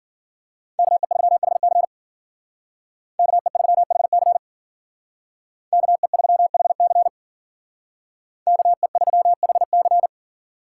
Morse code, heard as a keyed tone, about 52 wpm, 700 Hz. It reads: KE3HC KE3HC KE3HC KE3HC